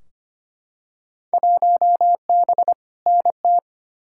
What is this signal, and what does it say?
Morse code, heard as a keyed tone, about 25 wpm, 700 Hz. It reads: 1B NT
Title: Field Day Exchanges Encore - 25wpm